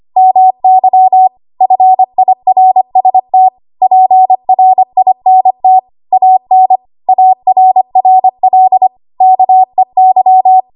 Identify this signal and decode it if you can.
Morse code, heard as a keyed tone, about 25 wpm, 750 Hz. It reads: MY FIRST PRINT AN ARRL KEY
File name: Archive / 25 WPM / 25WPM